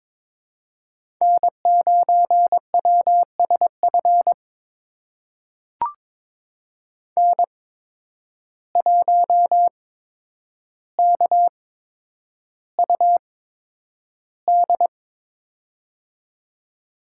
Morse code, heard as a keyed tone, about 22 wpm, spaced as for 9 wpm, 700 Hz. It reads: N9WSF N1KUD
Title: USA Call Signs - 5 Characters - 8x Character Spacing - 22wpm